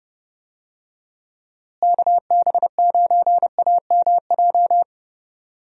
Morse code, heard as a keyed tone, about 30 wpm, 700 Hz. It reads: KB9AMJ
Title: USA Call Signs - 6 Characters - Repeated 3x - 30wpm